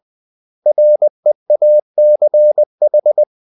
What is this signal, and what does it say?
Morse code, heard as a keyed tone, about 20 wpm, 600 Hz. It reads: REACH
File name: CW / Practice / 500MostCommonEnglishWords20